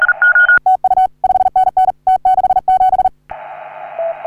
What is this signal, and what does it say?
Morse code, heard as a keyed tone, about 39 wpm, 750 Hz, with radio noise. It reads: TU 5NN T67
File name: CW / HA3OU RHOBCW REC